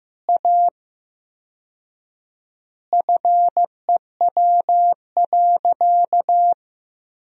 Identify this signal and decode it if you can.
Morse code, heard as a keyed tone, about 15 wpm, 700 Hz. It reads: A FEW.
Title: 4x Word-Spacing - Sentences from Top 300 Words - 15wpm